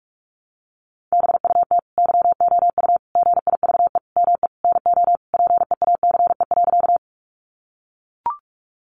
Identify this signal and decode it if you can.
Morse code, heard as a keyed tone, about 44 wpm, 700 Hz. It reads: BUT YOU GIVE ME NO PEACE.
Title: Sentences from Top 1500 Words 45wpm